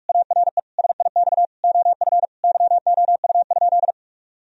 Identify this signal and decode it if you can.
Morse code, heard as a keyed tone, about 46 wpm, 700 Hz. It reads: ARE SIX OF YOU?
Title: Sentences from Top 1000 Words - Review Part 2 - 45wpm